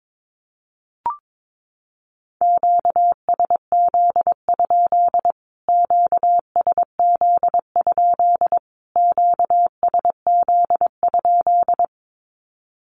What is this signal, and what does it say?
Morse code, heard as a keyed tone, about 22 wpm, 700 Hz. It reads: QSZ? QSZ? QSZ?
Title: Q-Codes Repeated 3x 22wpm